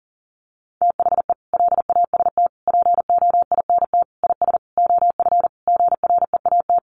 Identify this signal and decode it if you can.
Morse code, heard as a keyed tone, about 40 wpm, 700 Hz. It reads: THE LAST POINT IS OF GREAT
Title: Sentences from Top 1200 Words 40wpm